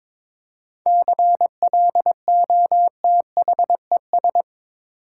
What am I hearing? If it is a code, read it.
Morse code, heard as a keyed tone, about 22 wpm, 700 Hz. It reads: CLOTHES
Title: Seven Letter Words - 22wpm